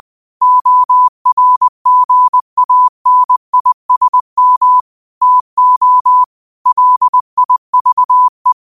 Morse code, heard as a keyed tone, about 20 wpm, 1000 Hz. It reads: ORGANISM TO LIVE
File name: qotdfiles / 20wpm